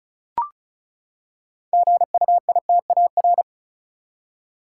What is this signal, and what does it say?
Morse code, heard as a keyed tone, about 35 wpm, 700 Hz. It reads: GUITAR
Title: Six Letter Words - 35wpm